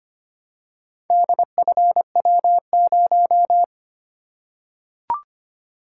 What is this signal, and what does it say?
Morse code, heard as a keyed tone, about 25 wpm, 700 Hz. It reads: DFW0
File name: practice / 25wpm / Four Letter-Number Combinations - 25wpm